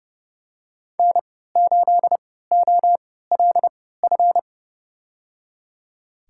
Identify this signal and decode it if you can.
Morse code, heard as a keyed tone, about 30 wpm, spaced as for 17 wpm, 700 Hz. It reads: N8OLF
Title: USA Call Signs - 5 Characters - 3x Character Spacing - 30wpm